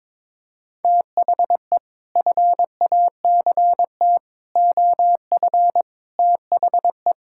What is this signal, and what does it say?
Morse code, heard as a keyed tone, about 22 wpm, 700 Hz. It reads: THE FACT OF THE
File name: practice / 22wpm / Sets of 6 Words - 22wpm